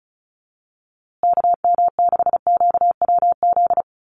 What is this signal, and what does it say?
Morse code, heard as a keyed tone, about 34 wpm, 700 Hz. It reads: KM6QWZ
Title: USA Call Signs Repeated 2x 35wpm